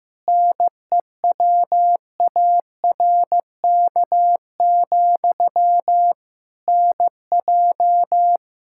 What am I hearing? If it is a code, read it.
Morse code, heard as a keyed tone, about 15 wpm, 700 Hz. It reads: NEWARK, NJ
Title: K1USN SST QSO Practice 15wpm